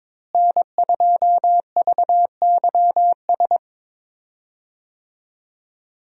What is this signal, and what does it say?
Morse code, heard as a keyed tone, about 22 wpm, 700 Hz. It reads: N2VYS
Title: USA Call Signs - 5 Characters - Repeated 2x - 22wpm